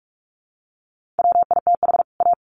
Morse code, heard as a keyed tone, about 44 wpm, 700 Hz. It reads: WITH A
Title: Sets of 2 Words 45wpm